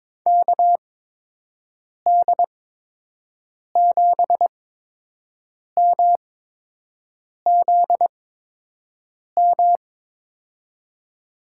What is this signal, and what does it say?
Morse code, heard as a keyed tone, about 22 wpm, spaced as for 6 wpm, 700 Hz. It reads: KD7MZM
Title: USA Call Signs - 8x Character Spacing - 22wpm